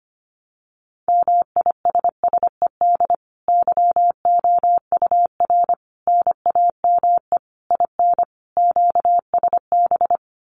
Morse code, heard as a keyed tone, about 25 wpm, 700 Hz. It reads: MISSED YOUR NAME IN QSB
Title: Common QSO Phrases 25wpm